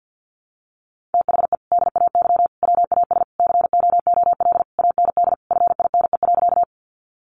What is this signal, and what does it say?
Morse code, heard as a keyed tone, about 51 wpm, 700 Hz. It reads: THE DAY WAS COOL AND FINE.
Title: Sentences from Top 1200 Words 50wpm